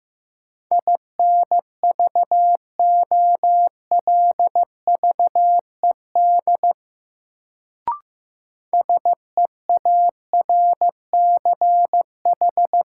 Morse code, heard as a keyed tone, about 15 wpm, 700 Hz. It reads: INVOLVED SEARCH
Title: Sentences from Top 800 Words - New Word Drill - 15wpm